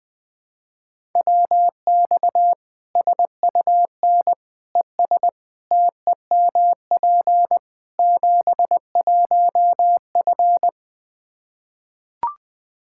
Morse code, heard as a keyed tone, about 21 wpm, 700 Hz. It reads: WX SUN ES TEMP 71F
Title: Practice QSO - Exchange 2 - Info - 20wpm